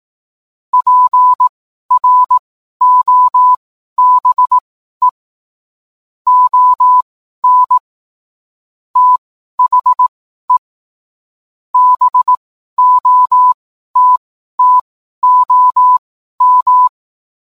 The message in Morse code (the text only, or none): PROBE ON THE BOTTOM